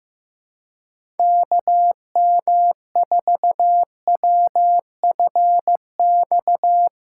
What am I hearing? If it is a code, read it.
Morse code, heard as a keyed tone, about 15 wpm, 700 Hz. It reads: KM4WFX
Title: USA Callsign DE Callsign - 6 Characters - 15wpm